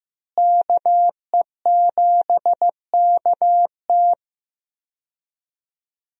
Morse code, heard as a keyed tone, about 15 wpm, 700 Hz. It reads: KE7KT
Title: USA Call Signs - 5 Characters - Repeated 3x - 15wpm